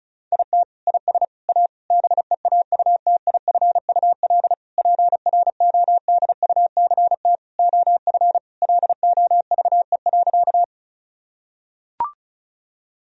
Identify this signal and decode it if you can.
Morse code, heard as a keyed tone, about 36 wpm, 700 Hz. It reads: IT IS A BEAUTIFUL PRODUCT OF LOVE.
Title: Sentences from Top 1000 Words - Review Part 2 - 35wpm